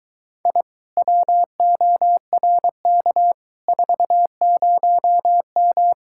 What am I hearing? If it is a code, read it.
Morse code, heard as a keyed tone, about 23 wpm, 700 Hz. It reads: I WORK 40M